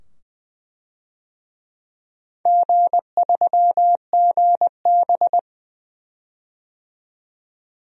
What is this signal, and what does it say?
Morse code, heard as a keyed tone, about 20 wpm, 700 Hz. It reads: G3GB
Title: DXCC-100 Call Signs - 4 Characters - 20wpm